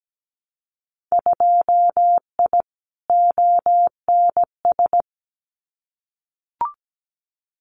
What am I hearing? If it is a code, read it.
Morse code, heard as a keyed tone, about 17 wpm, 700 Hz. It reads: 2I ONS
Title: Winter Field Day Exchanges Encore 17wpm